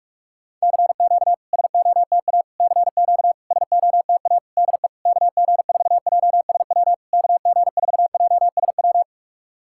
Morse code, heard as a keyed tone, about 46 wpm, 690 Hz. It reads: CQ SOTA CQ SOTA DE KG4JSW KG4JSW
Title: SOTA QSO Practice 45wpm